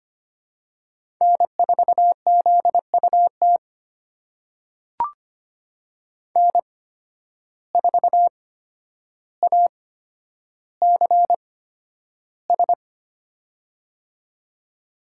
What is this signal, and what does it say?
Morse code, heard as a keyed tone, about 25 wpm, spaced as for 11 wpm, 700 Hz. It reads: N4ZUT N4ACS
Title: USA Call Signs - 5 Characters - 8x Character Spacing - 25wpm